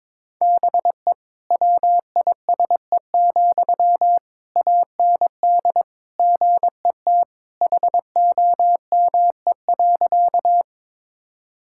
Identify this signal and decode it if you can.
Morse code, heard as a keyed tone, about 22 wpm, 700 Hz. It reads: BE WISE, AND GET HOME.
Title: Sentences from Top 1600 Words 22wpm